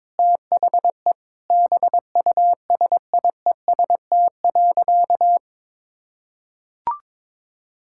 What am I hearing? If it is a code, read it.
Morse code, heard as a keyed tone, about 22 wpm, 700 Hz. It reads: THE BUSIEST.
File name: Sentences with 7-Word Phrases - 22wpm